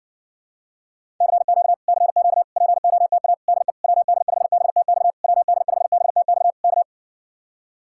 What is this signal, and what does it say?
Morse code, heard as a keyed tone, about 67 wpm, 700 Hz. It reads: CQ CQ POTA DE WD5BTX WD5BTX K